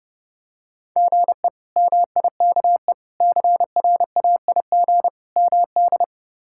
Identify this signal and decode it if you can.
Morse code, heard as a keyed tone, about 32 wpm, 700 Hz. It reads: GE MIKE CRAIG MD
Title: K1USN SST QSO Practice - 30wpm